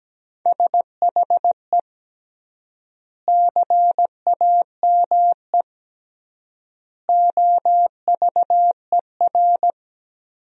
Morse code, heard as a keyed tone, about 17 wpm, 700 Hz. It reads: SHE CAME OVER
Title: 3x Word-Spacing - Sentences from Top 300 Words - 17wpm